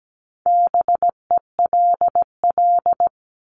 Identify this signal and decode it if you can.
Morse code, heard as a keyed tone, about 17 wpm, 700 Hz. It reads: BELL
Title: Top 1751-2000 Words 17wpm